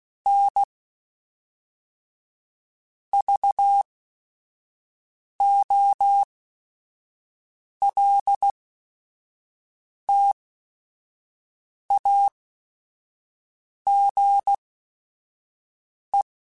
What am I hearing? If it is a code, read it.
Morse code, heard as a keyed tone, about 16 wpm, spaced as for 5 wpm, 800 Hz. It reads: N VOLTAGE
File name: Technical Subjects / Technical Subjects-mp3s / XC @5